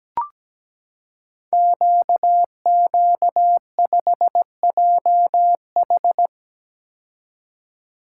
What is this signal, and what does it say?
Morse code, heard as a keyed tone, about 17 wpm, 690 Hz. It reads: QQ5JH
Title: Code Groups - 5 Letter:Numbers - 17wpm